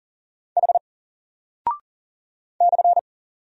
Morse code, heard as a keyed tone, about 38 wpm, 700 Hz. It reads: H /
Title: Single Character (Letter:Number:Punctuation:Prosign) Alt 40wpm